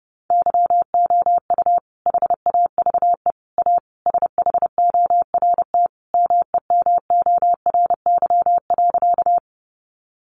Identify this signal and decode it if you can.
Morse code, heard as a keyed tone, about 28 wpm, 700 Hz. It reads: YOU HAVE A SHORT MEMORY.
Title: Sentences from Top 1300 Words 30wpm